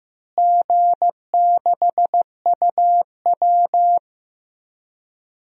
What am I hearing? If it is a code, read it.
Morse code, heard as a keyed tone, about 15 wpm, 700 Hz. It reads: G6UW
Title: DXCC-100 Call Signs - 4 Characters - Repeated 2x - 15wpm